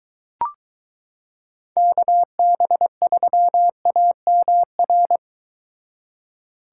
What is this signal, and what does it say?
Morse code, heard as a keyed tone, about 23 wpm, 700 Hz. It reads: KB3AMR